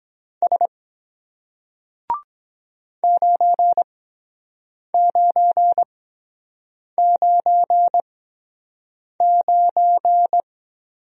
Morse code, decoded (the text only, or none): S 9999